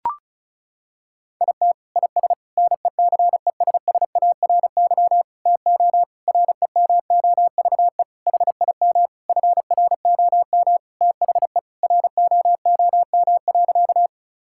Morse code, heard as a keyed tone, about 34 wpm, 700 Hz. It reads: IT IS NECESSARY TO REMOVE HIM FROM THE ROOM.
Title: Sentences from Top 1000 Words - Review Part 5 - 35wpm